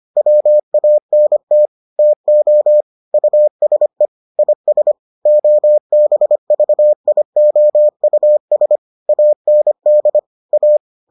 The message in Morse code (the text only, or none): WANT TO USE IS OBVIOUS AND A